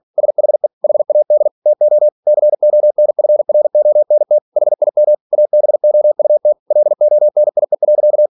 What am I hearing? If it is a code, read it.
Morse code, heard as a keyed tone, about 46 wpm, 600 Hz. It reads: SHE HAD TO CONFRONT HIM ABOUT LONIE.
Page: Never 47wpm